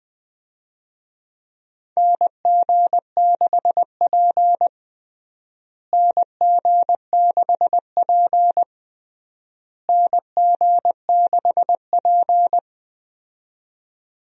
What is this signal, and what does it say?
Morse code, heard as a keyed tone, about 20 wpm, 700 Hz. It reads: NG6P NG6P NG6P
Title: USA Call Signs - 4 Characters - Repeated 3x - 20wpm